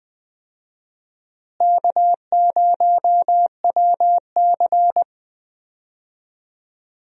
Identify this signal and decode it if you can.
Morse code, heard as a keyed tone, about 20 wpm, 700 Hz. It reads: K0WC
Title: USA Call Signs - 4 Characters - Repeated 3x - 20wpm